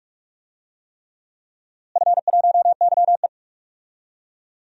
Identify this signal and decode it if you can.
Morse code, heard as a keyed tone, about 46 wpm, 700 Hz. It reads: F1YE
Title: DXCC-100 Call Signs - 4 Characters - Repeated 3x - 45wpm